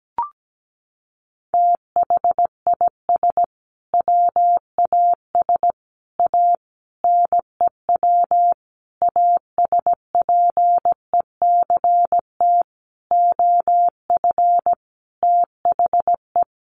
Morse code, heard as a keyed tone, about 17 wpm, 700 Hz. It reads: THIS WAS A NEW ASPECT OF THE
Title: Sentences from Top 1300 Words 17wpm